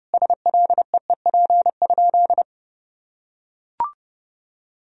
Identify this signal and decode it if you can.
Morse code, heard as a keyed tone, about 30 wpm, 700 Hz. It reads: SLEEP?
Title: Sentences from Top 500 Words and Abbreviations 30wpm